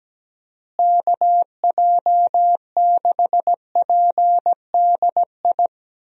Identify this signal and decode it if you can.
Morse code, heard as a keyed tone, about 17 wpm, 700 Hz. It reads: KJ6PDI
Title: USA Call Signs - 6 Characters - Repeated 3x - 17wpm